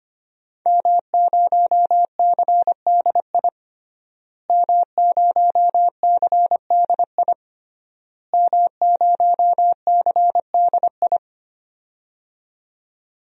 Morse code, heard as a keyed tone, about 25 wpm, 700 Hz. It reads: M0CDI M0CDI M0CDI
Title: DXCC-100 Call Signs - 5 Characters - Repeated 3x - 25wpm